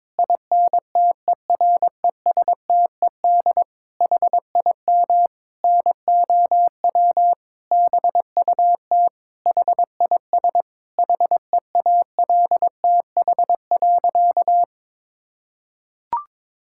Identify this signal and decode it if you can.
Morse code, heard as a keyed tone, about 22 wpm, 700 Hz. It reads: INTERESTED HIM NOW BUT HIS HEALTH.
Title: Sentences from Top 1000 Words - Review Part 6 - 22wpm